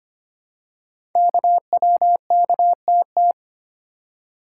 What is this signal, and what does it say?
Morse code, heard as a keyed tone, about 25 wpm, 700 Hz. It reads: KWKTT
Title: Code Groups - 5 Letters - 25wpm